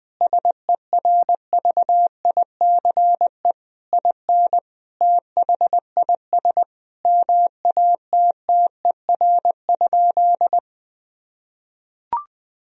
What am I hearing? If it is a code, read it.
Morse code, heard as a keyed tone, about 20 wpm, 700 Hz. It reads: SERVICE IN THIS MATTER?
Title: Sentences from Top 1000 Words - Review Part 4 - 20wpm